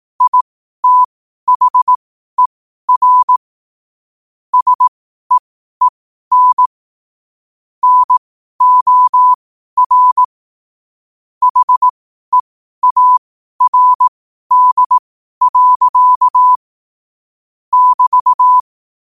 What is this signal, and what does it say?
Morse code, heard as a keyed tone, about 18 wpm, spaced as for 12 wpm, 1000 Hz. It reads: ITHER SEEN NOR HEARD. <BT>
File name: qotdfiles / 12wpm